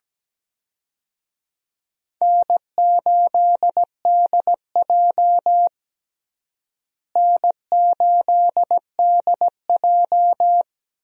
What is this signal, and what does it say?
Morse code, heard as a keyed tone, about 17 wpm, 700 Hz. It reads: N8DJ N8DJ
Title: USA Call Signs - 4 Characters - Repeated 2x - 17wpm